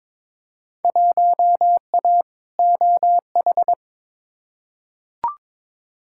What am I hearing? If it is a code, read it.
Morse code, heard as a keyed tone, about 22 wpm, 700 Hz. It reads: 1A OH
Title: Field Day Exchanges - 22wpm